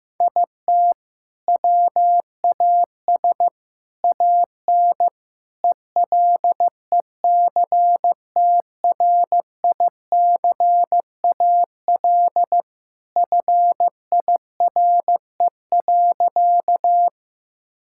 Morse code, heard as a keyed tone, about 15 wpm, 700 Hz. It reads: IT WAS AN ELECTRICAL FIRE.
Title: Sentences from Top 1000 Words - Review Part 6 - 15wpm